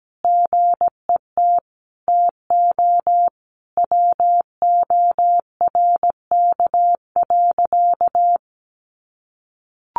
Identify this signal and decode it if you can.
Morse code, heard as a keyed tone, about 17 wpm, 700 Hz. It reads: GET TO WORK.
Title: Sentences from Top 600 Words 17wpm